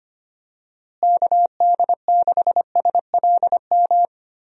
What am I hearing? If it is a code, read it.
Morse code, heard as a keyed tone, about 25 wpm, 700 Hz. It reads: KD6SLM
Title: USA Call Signs - 6 Characters - Repeated 2x - 25wpm